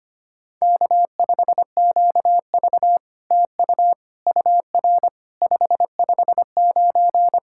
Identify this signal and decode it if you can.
Morse code, heard as a keyed tone, about 25 wpm, 700 Hz. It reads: K5QV TU UR 559
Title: SOTA QSO Practice 25wpm